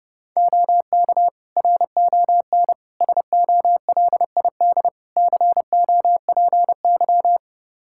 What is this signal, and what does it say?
Morse code, heard as a keyed tone, about 31 wpm, 700 Hz. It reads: OK RON SOLID COPY
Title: Practice QSO - Exchange 3 - More Info - 30wpm